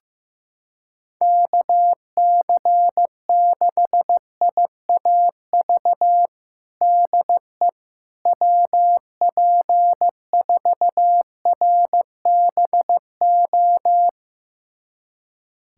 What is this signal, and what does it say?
Morse code, heard as a keyed tone, about 15 wpm, 700 Hz. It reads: KC6IAV DE WP4RBO
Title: USA Callsign DE Callsign 15wpm